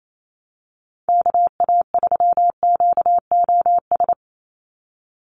Morse code, heard as a keyed tone, about 28 wpm, 700 Hz. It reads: KA3QOS
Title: USA Call Signs 28wpm